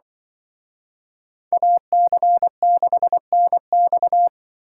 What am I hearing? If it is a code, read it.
Morse code, heard as a keyed tone, about 25 wpm, 700 Hz. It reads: AC6NX
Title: Speed Racing - USA 2x2 Call Signs 20wpm